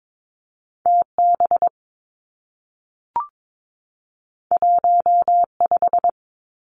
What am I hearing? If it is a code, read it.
Morse code, heard as a keyed tone, about 22 wpm, 700 Hz. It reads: T B 1 5
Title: Two Challenging Letter-Number Combinations 22wpm